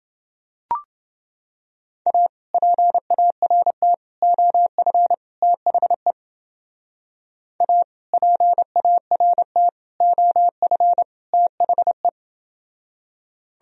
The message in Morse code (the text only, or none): A PART OF THE A PART OF THE